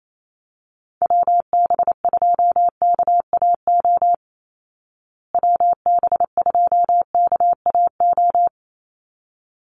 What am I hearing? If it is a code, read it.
Morse code, heard as a keyed tone, about 28 wpm, 700 Hz. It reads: WB2KAO WB2KAO
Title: Call Signs Repeated 3x - 28wpm